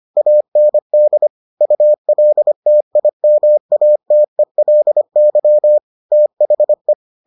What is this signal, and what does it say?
Morse code, heard as a keyed tone, about 25 wpm, 600 Hz. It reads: AND ULTIMATELY THE